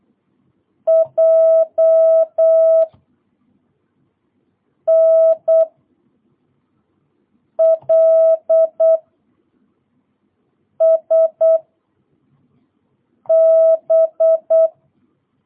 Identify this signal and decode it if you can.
Morse code, heard as a keyed tone, about 8 wpm, 660 Hz. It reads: J N L S B